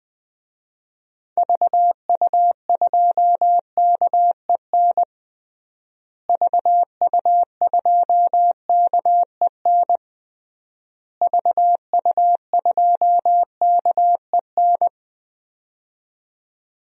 Morse code, heard as a keyed tone, about 20 wpm, 700 Hz. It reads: VU2KEN VU2KEN VU2KEN
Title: DXCC-100 Call Signs - 6 Characters - Repeated 3x - 20wpm